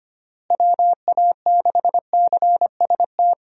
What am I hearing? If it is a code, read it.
Morse code, heard as a keyed tone, about 25 wpm, 700 Hz. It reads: WA6CST
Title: USA Call Signs - 6 Characters - Repeated 3x - 25wpm